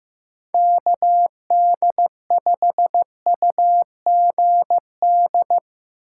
Morse code, heard as a keyed tone, about 15 wpm, 700 Hz. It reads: KD5UGD
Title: USA Call Signs - 6 Characters - 2x Character Spacing - 15wpm